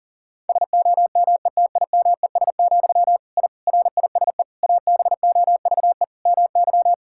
Morse code, heard as a keyed tone, about 40 wpm, 700 Hz. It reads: SOMETIMES, I RISE ABOVE MY
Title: Sentences from Top 1000 Words - Review Part 6 - 40wpm